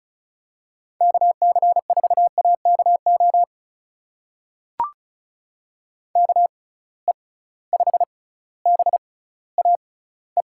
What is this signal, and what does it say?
Morse code, heard as a keyed tone, about 34 wpm, spaced as for 18 wpm, 700 Hz. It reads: KC4AKO KE5BAE
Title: USA Call Signs - 6 Characters - 6x Character Spacing - 35wpm